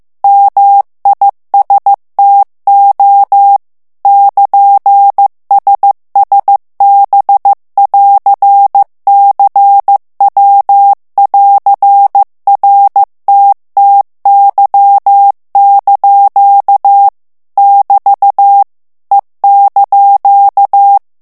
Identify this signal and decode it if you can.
Morse code, heard as a keyed tone, about 15 wpm, 800 Hz. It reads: MISTO <KN>SSB<AR>CW<AR>RTTY) <BT> E)
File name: morse mp3 / ITALIEN texte clair 2 vitesse 14 mots minute